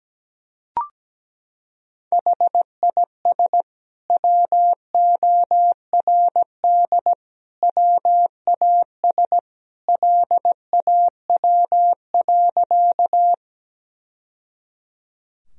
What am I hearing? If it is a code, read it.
Morse code, heard as a keyed tone, about 17 wpm, 700 Hz. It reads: HIS WORD WAS LAW.
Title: Sentences from Top 1000 Words - Review Part 3 - 17wpm